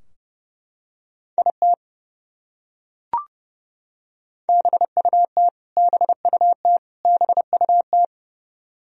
Morse code, heard as a keyed tone, about 30 wpm, 700 Hz. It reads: IT BUT BUT BUT
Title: Top 100 Words Repeated 30wpm